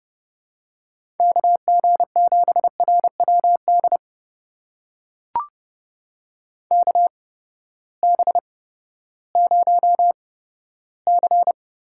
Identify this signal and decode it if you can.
Morse code, heard as a keyed tone, about 31 wpm, spaced as for 15 wpm, 700 Hz. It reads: KG7RWD KB0C
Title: USA Call Signs - 6 Characters - 8x Character Spacing - 30wpm